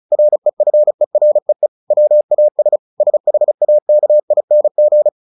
Morse code, heard as a keyed tone, about 34 wpm, 600 Hz. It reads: REFEREE WAS SHAKING
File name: the old man and the sea 2 35wpm